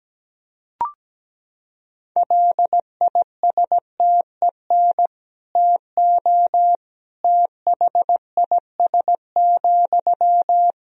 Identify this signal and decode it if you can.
Morse code, heard as a keyed tone, about 17 wpm, 700 Hz. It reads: LISTEN TO THIS,
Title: Sentences from Top 400 Words and Abbreviations 17wpm